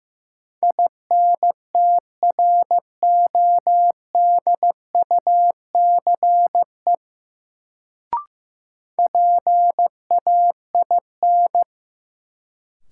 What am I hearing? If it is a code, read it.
Morse code, heard as a keyed tone, about 16 wpm, 700 Hz. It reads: INTRODUCE PAIN
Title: Sentences from Top 1600 Words - New Word Drill - 15wpm